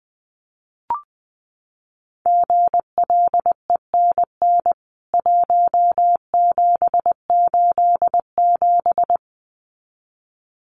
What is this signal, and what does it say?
Morse code, heard as a keyed tone, about 20 wpm, 700 Hz. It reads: GLENN 1787
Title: CWT Exchange 20wpm